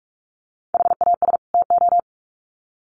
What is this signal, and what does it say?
Morse code, heard as a keyed tone, about 44 wpm, 700 Hz. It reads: HAS TO
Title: Sets of 2 Words 45wpm